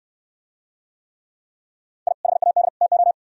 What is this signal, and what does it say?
Morse code, heard as a keyed tone, about 63 wpm, 700 Hz. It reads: I HAD TO